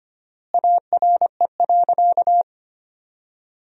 Morse code, heard as a keyed tone, about 25 wpm, 700 Hz. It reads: ARE.
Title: 4x Word-Spacing - Sentences from Top 100 Words - 25wpm